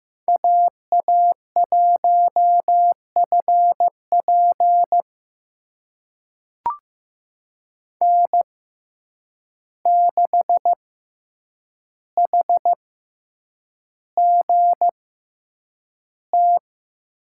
Morse code, heard as a keyed tone, about 15 wpm, spaced as for 8 wpm, 700 Hz. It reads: AA1FP N6HGT
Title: USA Call Signs - 5 Characters - 6x Character Spacing - 15wpm